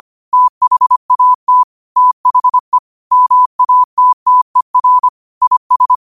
Morse code, heard as a keyed tone, about 25 wpm, 1000 Hz. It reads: THAT THE MATTER IS
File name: qotdfiles / 25wpm